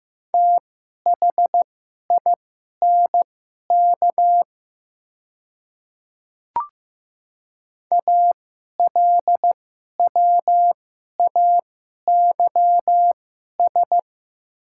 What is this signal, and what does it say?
Morse code, heard as a keyed tone, about 15 wpm, spaced as for 11 wpm, 690 Hz. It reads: THINK ALWAYS
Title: 2x Character Spacing - Top 300 Words - 15wpm